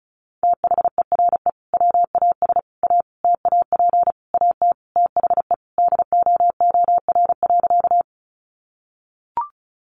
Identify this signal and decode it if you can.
Morse code, heard as a keyed tone, about 34 wpm, 700 Hz. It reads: THERE WAS A TAP AT THE DOOR.
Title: Sentences from Top 1400 Words 35wpm